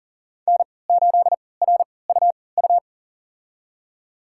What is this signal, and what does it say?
Morse code, heard as a keyed tone, about 40 wpm, spaced as for 23 wpm, 700 Hz. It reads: N8RUU
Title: USA Call Signs - 5 Characters - 3x Character Spacing - 40wpm